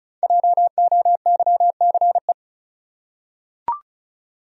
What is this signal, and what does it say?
Morse code, heard as a keyed tone, about 36 wpm, 700 Hz. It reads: JOYCE
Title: USA Common Names - Max 5 Letters - 35wpm